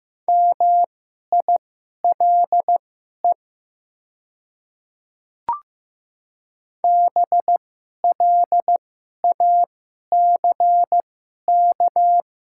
Morse code, heard as a keyed tone, about 15 wpm, spaced as for 11 wpm, 700 Hz. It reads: MILE BLACK
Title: 2x Character Spacing - Top 500 Words - 15wpm